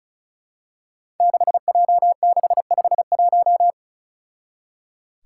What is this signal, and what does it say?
Morse code, heard as a keyed tone, about 36 wpm, 700 Hz. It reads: 6J651
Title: Five Challenging Letter-Number Combinations 35wpm